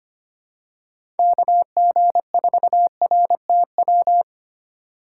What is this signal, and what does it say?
Morse code, heard as a keyed tone, about 25 wpm, 700 Hz. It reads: KG4RTW
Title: USA Call Signs - 6 Characters - 6x Character Spacing - 25wpm